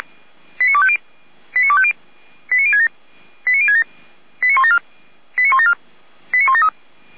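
Five-tone selective call (ZVEI), heard as quick runs of five tones, tones of about 70 ms. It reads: 89250 89250 89067 89067 89174 89174 89173
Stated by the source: ZVEI five-tone selective call 89240 received via SDR, hosted by feuerwehr-orlamuende fire brigade.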